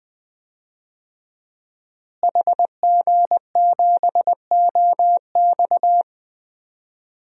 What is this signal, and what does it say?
Morse code, heard as a keyed tone, about 20 wpm, 700 Hz. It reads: HG7OX
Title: DXCC-100 Call Signs - 5 Characters - Repeated 2x - 20wpm